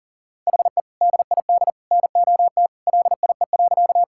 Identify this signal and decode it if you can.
Morse code, heard as a keyed tone, about 40 wpm, 700 Hz. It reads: HE DID NOT LIE.
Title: Sentences from Top 1600 Words 40wpm